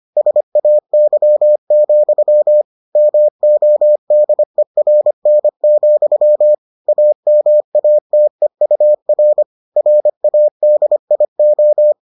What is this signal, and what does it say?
Morse code, heard as a keyed tone, about 25 wpm, 600 Hz. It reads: SAY, MODERN, AMATEUR RADIO